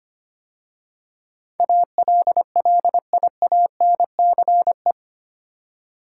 Morse code, heard as a keyed tone, about 25 wpm, 700 Hz. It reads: ALLIANCE
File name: 8 Letter Words - First 3 Letters Given - 25wpm